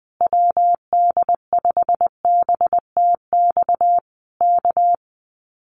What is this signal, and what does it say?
Morse code, heard as a keyed tone, about 20 wpm, 700 Hz. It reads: WD5BTX K
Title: Common QSO Phrases Encore 20wpm